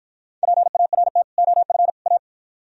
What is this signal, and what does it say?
Morse code, heard as a keyed tone, about 52 wpm, 700 Hz. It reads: PART OF A